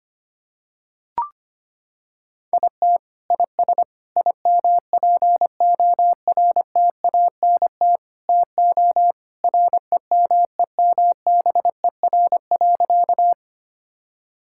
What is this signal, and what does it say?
Morse code, heard as a keyed tone, about 25 wpm, 700 Hz. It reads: IT IS IMPORTANT TO REMEMBER.
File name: practice / 25wpm / Sentences with 4-Word Phrases - 25wpm